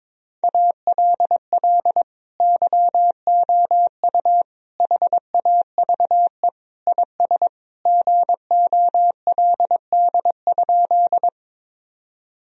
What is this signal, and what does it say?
Morse code, heard as a keyed tone, about 22 wpm, 700 Hz. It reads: ALL YOU HAVE IS GOLD?
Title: Sentences from Top 1000 Words - Review Part 3 - 22wpm